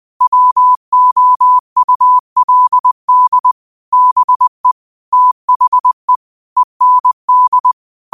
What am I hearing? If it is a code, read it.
Morse code, heard as a keyed tone, about 20 wpm, 1000 Hz. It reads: WOULD BE THE END